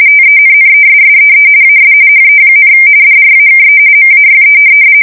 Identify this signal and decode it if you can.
RTTY, about 45.45 baud. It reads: TENNESSEE AS HE, DRAKE MAYE
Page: 170 Shift 2125/2295